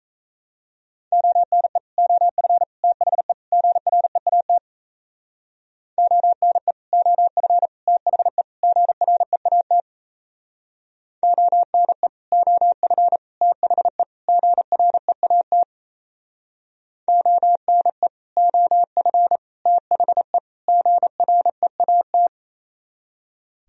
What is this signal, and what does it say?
Morse code, heard as a keyed tone, about 34 wpm, 700 Hz. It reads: ONE OF THE GREAT ONE OF THE GREAT ONE OF THE GREAT ONE OF THE GREAT
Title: Speed-Racing - Sets of 4 Words 28wpm